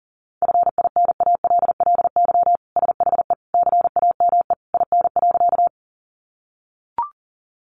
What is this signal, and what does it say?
Morse code, heard as a keyed tone, about 40 wpm, 700 Hz. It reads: FINALLY SHE CAME IN.
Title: Sentences from Top 1000 Words 40wpm